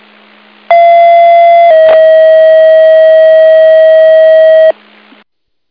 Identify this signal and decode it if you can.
Two-tone page, heard as a one-second tone, then a three-second tone, about 680 then 640 Hz. It is heard from 0.7 to 4.7 s.